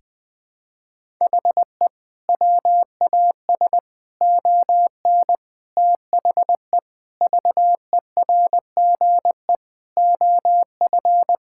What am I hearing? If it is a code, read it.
Morse code, heard as a keyed tone, about 21 wpm, 700 Hz. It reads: HE WAS ON THE VERGE OF
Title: Sentences with 4-Word Phrases - 20wpm